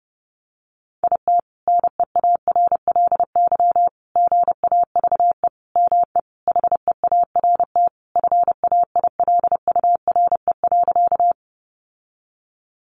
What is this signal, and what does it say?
Morse code, heard as a keyed tone, about 30 wpm, 700 Hz. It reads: IT NEARLY GAVE ME HEART FAILURE.
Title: Sentences from Top 1300 Words 30wpm